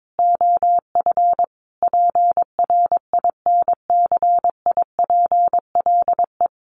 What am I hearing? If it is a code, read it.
Morse code, heard as a keyed tone, about 22 wpm, 700 Hz. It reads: OF PRINCIPLE
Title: Sentences from Top 1400 Words 22wpm